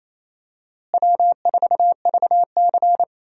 Morse code, heard as a keyed tone, about 28 wpm, 700 Hz. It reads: W4VC
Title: USA Call Signs - 4 Characters - Repeated 3x - 28wpm